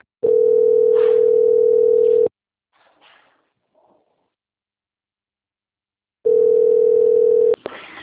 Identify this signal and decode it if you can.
Ringback tone (North America), 440 + 480 Hz.